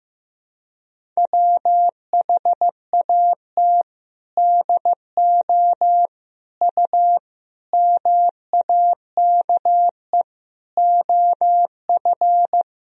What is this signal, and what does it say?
Morse code, heard as a keyed tone, about 15 wpm, 700 Hz. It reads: WHAT DO U MAKE OF
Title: Sentences from Top 100 Words and Abbreviations 15wpm